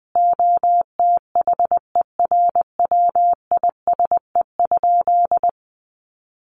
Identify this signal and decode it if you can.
Morse code, heard as a keyed tone, about 20 wpm, 700 Hz. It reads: OTHERWISE?
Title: Sentences from Top 1200 Words 20wpm